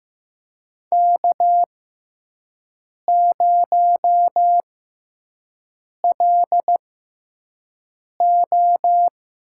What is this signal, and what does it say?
Morse code, heard as a keyed tone, about 15 wpm, spaced as for 5 wpm, 700 Hz. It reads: K0LO
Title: USA Call Signs - 4 Characters - 6x Character Spacing - 15wpm